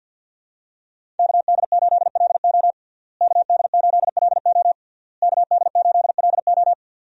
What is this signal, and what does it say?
Morse code, heard as a keyed tone, about 46 wpm, 700 Hz. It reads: KD8LO KD8LO KD8LO
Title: USA Call Signs - 5 Characters - Repeated 3x - 50wpm